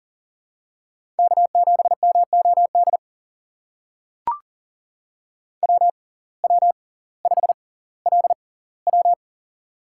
Morse code, heard as a keyed tone, about 40 wpm, spaced as for 20 wpm, 700 Hz. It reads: K7MOD WW5LW
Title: USA Call Signs - 6x Character Spacing - 40wpm